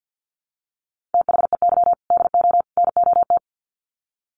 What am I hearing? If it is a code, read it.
Morse code, heard as a keyed tone, about 48 wpm, 700 Hz. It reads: THEY DO NOT
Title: Sets of 3 Words 50wpm